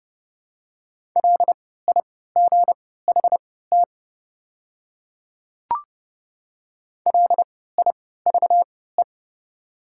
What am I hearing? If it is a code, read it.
Morse code, heard as a keyed tone, about 30 wpm, spaced as for 17 wpm, 700 Hz. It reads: LIGHT LIVE
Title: 3x Character Spacing - Top 200 Words - 30wpm